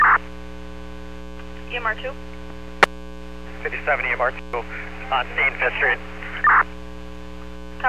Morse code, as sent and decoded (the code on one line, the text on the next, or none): none